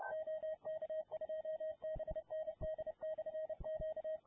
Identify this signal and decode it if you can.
Morse code, heard as a keyed tone, about 30 wpm, 630 Hz. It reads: OK2BND/Q